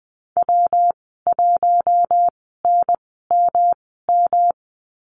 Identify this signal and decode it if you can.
Morse code, heard as a keyed tone, about 20 wpm, spaced as for 14 wpm, 700 Hz. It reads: W1NMM
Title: 161 Lesson 26g - Letter g - Call Signs 20wpm